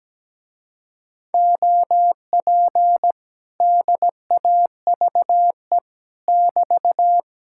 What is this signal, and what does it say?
Morse code, heard as a keyed tone, about 17 wpm, 700 Hz. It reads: OP DAVE <BT>
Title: Speed-Racing Common QSO Phrases - 17wpm